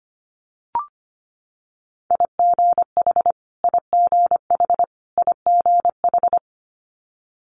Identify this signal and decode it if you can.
Morse code, heard as a keyed tone, about 25 wpm, 700 Hz. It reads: IGH IGH IGH
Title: Multi-letter Phonemes Repeated 3x 25wpm